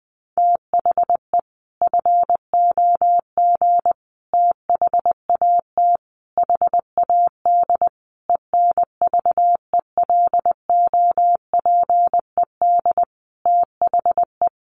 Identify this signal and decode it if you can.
Morse code, heard as a keyed tone, about 20 wpm, 700 Hz. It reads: THE FOG THAT HAD ENVELOPED THE
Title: A Sound Of Thunder 20wpm